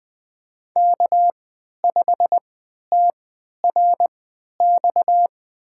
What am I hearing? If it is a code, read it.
Morse code, heard as a keyed tone, about 20 wpm, spaced as for 11 wpm, 700 Hz. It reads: K5TRX
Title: USA Call Signs - 5 Characters - 3x Character Spacing - 20wpm